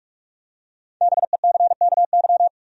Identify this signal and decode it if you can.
Morse code, heard as a keyed tone, about 44 wpm, 700 Hz. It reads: BECKY
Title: USA Common Names - Max 5 Letters - 45wpm